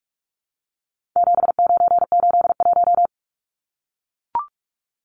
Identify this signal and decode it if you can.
Morse code, heard as a keyed tone, about 44 wpm, 700 Hz. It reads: 7981
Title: Numbers - 4-Digit 45wpm